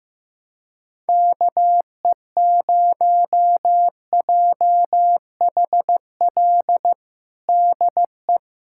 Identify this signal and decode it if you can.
Morse code, heard as a keyed tone, about 15 wpm, 700 Hz. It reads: KE0JHL DE
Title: USA Callsign DE Callsign 15wpm